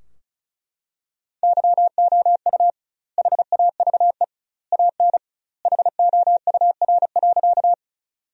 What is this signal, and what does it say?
Morse code, heard as a keyed tone, about 34 wpm, 700 Hz. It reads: YOU HAVE AN HOUR.
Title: 2x Word-Spacing - Sentences from Top 400 Words - 35wpm